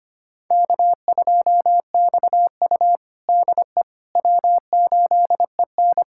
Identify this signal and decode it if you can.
Morse code, heard as a keyed tone, about 25 wpm, 700 Hz. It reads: K2XU DE W8EN
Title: USA Callsign DE Callsign - 4 Characters - 25wpm